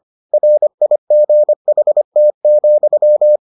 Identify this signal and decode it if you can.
Morse code, heard as a keyed tone, about 25 wpm, 600 Hz. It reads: RIGHT,